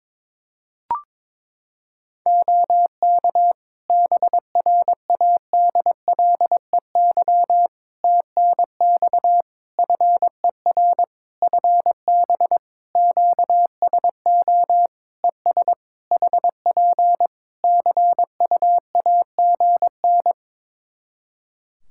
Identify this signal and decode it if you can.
Morse code, heard as a keyed tone, about 22 wpm, 700 Hz. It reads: OK BRADLEY TNX FER FB QSO ES HP CUAGN
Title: Practice QSO - Exchange 4 - Ending - 22wpm